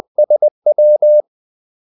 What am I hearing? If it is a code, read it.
Morse code, heard as a keyed tone, about 20 wpm, 600 Hz. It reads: SW
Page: Kaldesignaler 20 wpm